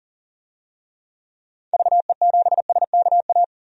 Morse code, heard as a keyed tone, about 40 wpm, 700 Hz. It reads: VE7SKA
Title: DXCC-100 Call Signs - 6 Characters - 40wpm